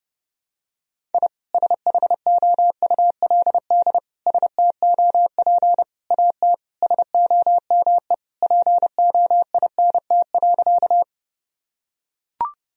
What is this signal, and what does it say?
Morse code, heard as a keyed tone, about 32 wpm, 700 Hz. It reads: I SHOULD STOP AT SOME POINT.
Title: Sentences from Top 1000 Words - Review Part 1 - 30wpm